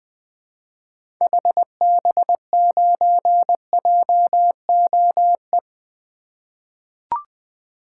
Morse code, heard as a keyed tone, about 21 wpm, 700 Hz. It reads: HB9JOE
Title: 2023 SOTA Activator Call Signs - 20wpm